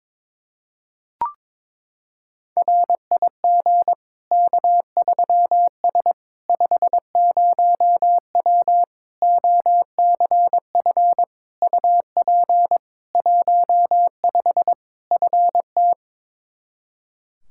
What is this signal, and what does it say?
Morse code, heard as a keyed tone, about 22 wpm, 700 Hz. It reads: RIG K3S 50W OCF UP 15 FT
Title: Practice QSO - Exchange 2 - Info - 22wpm